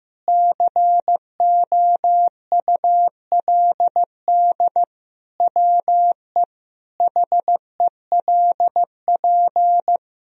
Morse code, heard as a keyed tone, about 15 wpm, 700 Hz. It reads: COULD WE HELP
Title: Sentences from Top 1000 Words - Review Part 3 - 15wpm